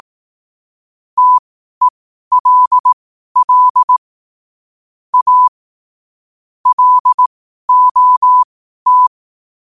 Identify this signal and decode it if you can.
Morse code, heard as a keyed tone, about 18 wpm, spaced as for 12 wpm, 1000 Hz. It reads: TELL A LOT
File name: qotdfiles / 12wpm